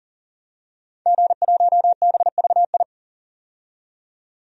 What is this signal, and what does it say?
Morse code, heard as a keyed tone, about 40 wpm, 700 Hz. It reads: G1BVI
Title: DXCC-100 Call Signs - 5 Characters - 40wpm